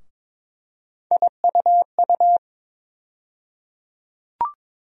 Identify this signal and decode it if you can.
Morse code, heard as a keyed tone, about 22 wpm, 700 Hz. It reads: IUU
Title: USA Call Sign Postfixes - 3 Letters 22wpm